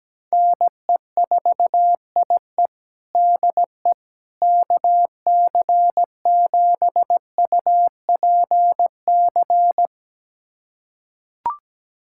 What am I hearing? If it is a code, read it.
Morse code, heard as a keyed tone, about 17 wpm, 700 Hz. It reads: NE4IE DE KC7UPC
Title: USA Callsign DE Callsign 17wpm